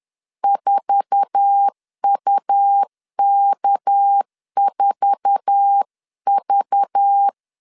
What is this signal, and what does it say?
Morse code, heard as a keyed tone, about 10 wpm, 800 Hz. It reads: 4UK4V
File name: LEZIONI / 8 YJ45 / 8 YJ4 10wpm 2633